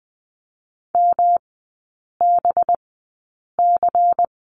Morse code, heard as a keyed tone, about 19 wpm, spaced as for 8 wpm, 700 Hz. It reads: MBC
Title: 184 Lesson 30e - Letter b - Three Characters 20wpm